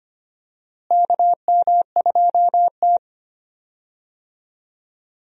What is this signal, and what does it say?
Morse code, heard as a keyed tone, about 25 wpm, 700 Hz. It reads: KM2T
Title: USA Call Signs - 4 Characters - Repeated 2x - 25wpm